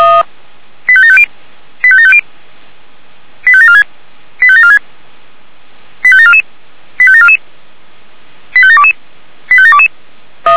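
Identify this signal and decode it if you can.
Five-tone selective call (ZVEI), heard as quick runs of five tones, tones of about 70 ms. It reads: 85750 85750 85746 85746 85744 85744 85722 85722
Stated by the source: ZVEI five-tone selective call received on SDR.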